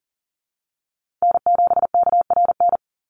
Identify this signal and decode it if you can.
Morse code, heard as a keyed tone, about 37 wpm, 700 Hz. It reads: N7KRN
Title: NA SOTA Chaser Callsigns 40wpm